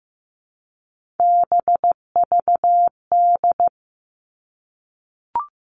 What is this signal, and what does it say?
Morse code, heard as a keyed tone, about 15 wpm, 700 Hz. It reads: BVD
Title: Three Challenging Letter-Number Combinations 15wpm